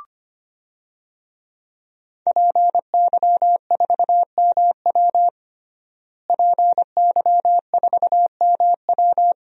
Morse code, heard as a keyed tone, about 25 wpm, 700 Hz. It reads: PY4MW PY4MW
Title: DXCC-100 Call Signs - 5 Characters - Repeated 3x - 25wpm